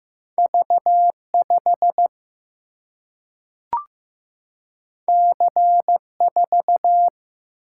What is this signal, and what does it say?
Morse code, heard as a keyed tone, about 15 wpm, 700 Hz. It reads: V 5 C 4
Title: Single-Letter Single-Number 15wpm